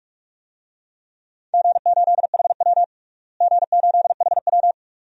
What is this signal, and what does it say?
Morse code, heard as a keyed tone, about 42 wpm, 700 Hz. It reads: G8HW G8HW
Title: DXCC-100 Call Signs - 4 Characters - Repeated 2x - 45wpm